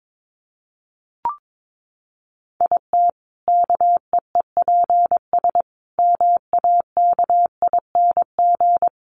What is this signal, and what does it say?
Morse code, heard as a keyed tone, about 22 wpm, 700 Hz. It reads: IT KEEPS MAKING
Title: Additional Easy Phrases 22wpm